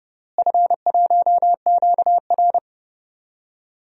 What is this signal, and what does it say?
Morse code, heard as a keyed tone, about 29 wpm, 700 Hz. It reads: F1QR
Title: DXCC-100 Call Signs - 4 Characters - Repeated 3x - 30wpm